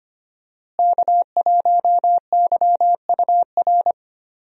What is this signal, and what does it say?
Morse code, heard as a keyed tone, about 25 wpm, 700 Hz. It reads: K1YUR
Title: USA Call Signs - 5 Characters - Repeated 3x - 25wpm